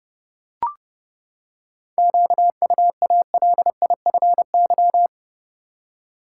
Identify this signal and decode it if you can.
Morse code, heard as a keyed tone, about 30 wpm, 700 Hz. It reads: QUALIFY
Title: Seven Letter Words - 30wpm